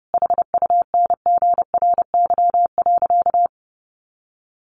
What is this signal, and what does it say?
Morse code, heard as a keyed tone, about 28 wpm, 700 Hz. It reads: HUNGRY.
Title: A Sound Of Thunder 30wpm